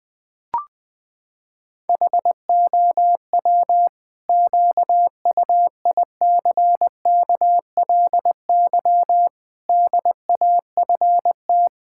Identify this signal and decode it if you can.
Morse code, heard as a keyed tone, about 20 wpm, 700 Hz. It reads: HOW QUICKLY DAFT
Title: Pangram Challenge v2 - 20wpm